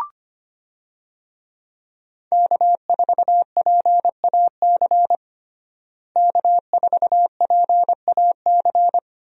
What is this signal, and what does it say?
Morse code, heard as a keyed tone, about 25 wpm, 700 Hz. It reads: K4PAC K4PAC
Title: USA Call Signs - 5 Characters - Repeated 3x - 25wpm